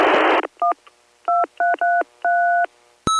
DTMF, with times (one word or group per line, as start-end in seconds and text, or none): 0.6-0.7 1
1.3-2.7 2333